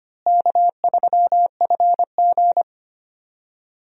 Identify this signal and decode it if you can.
Morse code, heard as a keyed tone, about 25 wpm, 700 Hz. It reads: K3FG
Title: USA Call Signs - 4 Characters - Repeated 2x - 25wpm